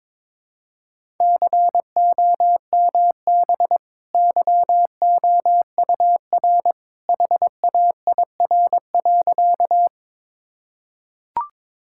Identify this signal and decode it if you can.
Morse code, heard as a keyed tone, about 22 wpm, 700 Hz. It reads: COMB YOUR HAIR.
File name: practice / 22wpm / Easy Sentences with Hints - 22wpm